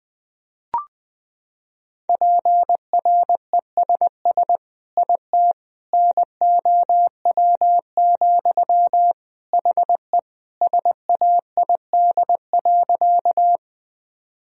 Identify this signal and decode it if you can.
Morse code, heard as a keyed tone, about 21 wpm, 700 Hz. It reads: PRESS IT NOW, HE SAID.
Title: Sentences from Top 1000 Words - Review Part 1 - 20wpm